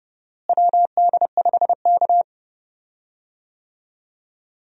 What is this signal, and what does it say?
Morse code, heard as a keyed tone, about 30 wpm, 700 Hz. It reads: WD5K
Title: USA Call Signs - 4 Characters - Repeated 3x - 30wpm